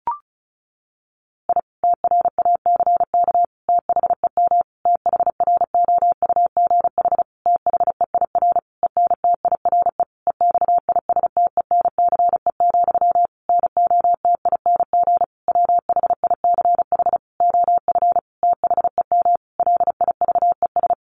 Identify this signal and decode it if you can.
Morse code, heard as a keyed tone, about 34 wpm, 700 Hz. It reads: I TRACK THEM THROUGH THEIR ENTIRE EXISTENCE, NOTING WHICH OF THEM LIVES
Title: A Sound Of Thunder 35wpm